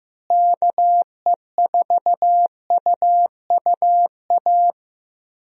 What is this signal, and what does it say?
Morse code, heard as a keyed tone, about 15 wpm, 700 Hz. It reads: KE4UUA
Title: USA Call Signs - 6 Characters - Repeated 2x - 15wpm